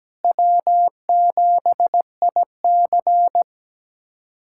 Morse code, heard as a keyed tone, about 17 wpm, 700 Hz. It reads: W7IC
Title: USA Callsign DE Callsign - 4 Characters - 17wpm